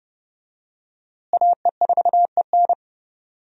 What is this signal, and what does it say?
Morse code, heard as a keyed tone, about 30 wpm, 700 Hz. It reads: AE4EN
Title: USA Call Signs - 5 Characters - Repeated 3x - 30wpm